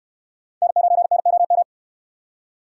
Morse code, heard as a keyed tone, about 66 wpm, 700 Hz. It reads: N9NOM